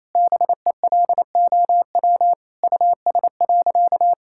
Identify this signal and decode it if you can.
Morse code, heard as a keyed tone, about 28 wpm, 700 Hz. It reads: BELOW US.
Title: Sentences from Top 1000 Words - Review Part 5 - 28wpm